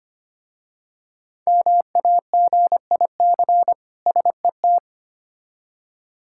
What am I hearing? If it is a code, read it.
Morse code, heard as a keyed tone, about 25 wpm, 700 Hz. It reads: MAGIC SET
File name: practice / 25wpm / Difficult Words with Hints - 25wpm